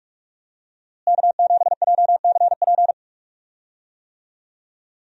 Morse code, heard as a keyed tone, about 46 wpm, 700 Hz. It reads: K7JCP
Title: USA Call Signs - 5 Characters - Repeated 3x - 45wpm